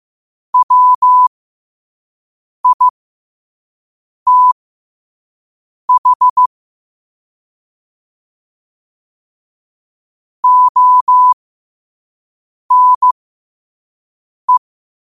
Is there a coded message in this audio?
Yes: Morse code from 0.5 to 14.6 s, heard as a keyed tone, about 15 wpm, spaced as for 5 wpm, 1000 Hz. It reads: WITH ONE